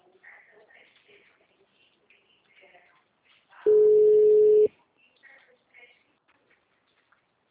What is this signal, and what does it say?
Ringback tone (Europe, 425 Hz), with speech.